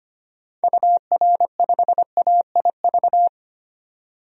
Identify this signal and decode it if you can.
Morse code, heard as a keyed tone, about 25 wpm, 700 Hz. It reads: UR5AIV
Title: DXCC-100 Call Signs - 6 Characters - 25wpm